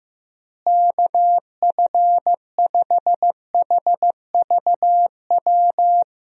Morse code, heard as a keyed tone, about 15 wpm, 700 Hz. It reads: KF5HVW
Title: USA Call Signs - 6 Characters - Repeated 3x - 15wpm